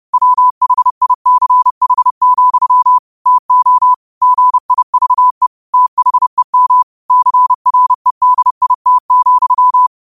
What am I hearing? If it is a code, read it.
Morse code, heard as a keyed tone, about 30 wpm, 1000 Hz. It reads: WHICH, TO GIVE THEM CREDIT,